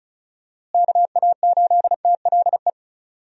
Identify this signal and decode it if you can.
Morse code, heard as a keyed tone, about 34 wpm, 700 Hz. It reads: KA8TLE
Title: USA Call Signs - 6 Characters - Repeated 2x - 35wpm